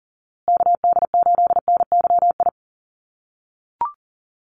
Morse code, heard as a keyed tone, about 38 wpm, 700 Hz. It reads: KD8NYI
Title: USA Call Signs Repeated 3x 40wpm